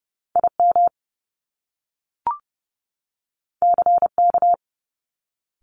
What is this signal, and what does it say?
Morse code, heard as a keyed tone, about 30 wpm, 700 Hz. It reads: IM CK
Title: Top Two Letter Combinations in Words 30wpm